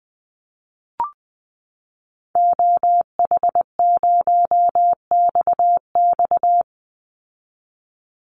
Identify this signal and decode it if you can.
Morse code, heard as a keyed tone, about 20 wpm, 700 Hz. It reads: OH0XX
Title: Call Signs Repeated 3x - 20wpm